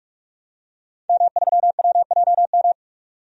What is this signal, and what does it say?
Morse code, heard as a keyed tone, about 41 wpm, 700 Hz. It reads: M3WJM
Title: DXCC-100 Call Signs - 5 Characters - 45wpm